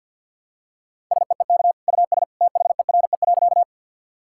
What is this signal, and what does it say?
Morse code, heard as a keyed tone, about 50 wpm, 700 Hz. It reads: SEEK US THERE.
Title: Sentences from Top 1600 Words 50wpm